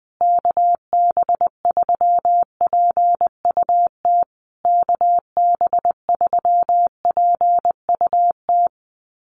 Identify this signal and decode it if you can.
Morse code, heard as a keyed tone, about 20 wpm, 700 Hz. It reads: KB3PUT KB3PUT
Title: USA Call Signs Repeated 3x 20wpm